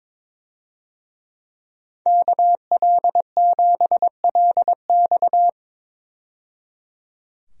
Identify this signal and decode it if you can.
Morse code, heard as a keyed tone, about 22 wpm, 700 Hz. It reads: KL7LX
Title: DXCC-100 Call Signs - 5 Characters - 22wpm